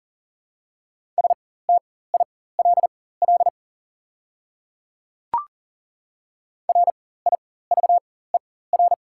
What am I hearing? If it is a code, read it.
Morse code, heard as a keyed tone, about 40 wpm, spaced as for 19 wpm, 690 Hz. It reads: STILL RIVER
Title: 4x Character Spacing - Top 500 Words - 40wpm